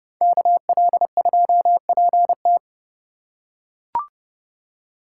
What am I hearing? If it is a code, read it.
Morse code, heard as a keyed tone, about 30 wpm, 700 Hz. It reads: KL2PT
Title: USA Call Signs - 4x Character Spacing - 30wpm